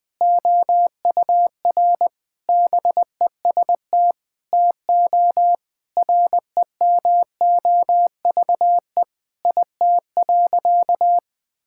Morse code, heard as a keyed tone, about 21 wpm, 700 Hz. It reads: OUR BEST TO REMOVE IT.
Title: Sentences from Top 1000 Words - Review Part 5 - 20wpm